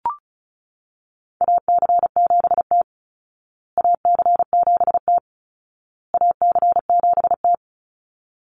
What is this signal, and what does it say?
Morse code, heard as a keyed tone, about 34 wpm, 700 Hz. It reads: AC7T AC7T AC7T
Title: Call Signs Repeated 2x - 35wpm